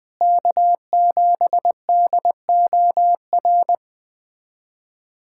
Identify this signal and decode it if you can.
Morse code, heard as a keyed tone, about 20 wpm, 700 Hz. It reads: K7DOR
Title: USA Callsign DE Callsign - 5 Characters - 20wpm